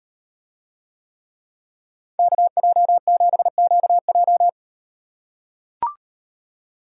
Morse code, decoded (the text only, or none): KJ7QJ